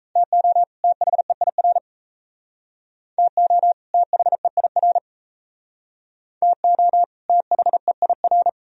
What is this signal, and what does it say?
Morse code, heard as a keyed tone, about 36 wpm, 700 Hz. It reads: TO THEIR TO THEIR TO THEIR